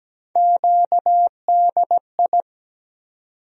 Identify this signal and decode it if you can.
Morse code, heard as a keyed tone, about 17 wpm, 700 Hz. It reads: QDI
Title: USA Call Sign Postfixes - 3 Letters 17wpm